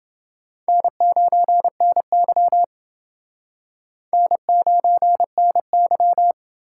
Morse code, heard as a keyed tone, about 27 wpm, 700 Hz. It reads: N9NY N9NY